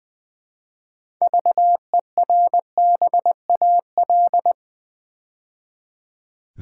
Morse code, heard as a keyed tone, about 20 wpm, 700 Hz. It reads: VERBAL
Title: Build A Word - 2 Letters At A Time - 6-Letter Words - 20wpm